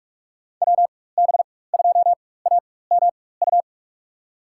Morse code, heard as a keyed tone, about 43 wpm, spaced as for 21 wpm, 690 Hz. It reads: WB2AMU
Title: USA Call Signs - 4x Character Spacing - 45wpm